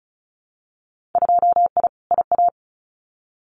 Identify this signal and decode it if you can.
Morse code, heard as a keyed tone, about 36 wpm, 700 Hz. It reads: 2I IA
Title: Winter Field Day Exchanges Encore 35wpm